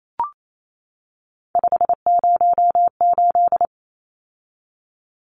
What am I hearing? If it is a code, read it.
Morse code, heard as a keyed tone, about 28 wpm, 700 Hz. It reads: 508
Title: Numbers - 3-Digit 28wpm